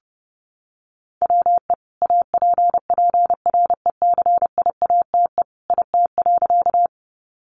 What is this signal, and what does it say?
Morse code, heard as a keyed tone, about 30 wpm, 700 Hz. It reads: WE APPRECIATE IT.
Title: Sentences from Top 1300 Words 30wpm